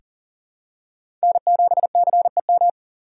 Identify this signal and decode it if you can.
Morse code, heard as a keyed tone, about 40 wpm, 700 Hz. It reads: N7CEM
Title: USA Call Signs - 6x Character Spacing - 40wpm